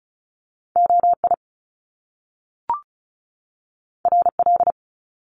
Morse code, heard as a keyed tone, about 36 wpm, 700 Hz. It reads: OI RL
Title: Top Two Letter Combinations in Words 35wpm